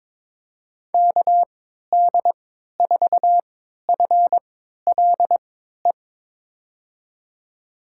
Morse code, heard as a keyed tone, about 22 wpm, spaced as for 13 wpm, 700 Hz. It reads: KD4FLE
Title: USA Call Signs - 6 Characters - 3x Character Spacing - 22wpm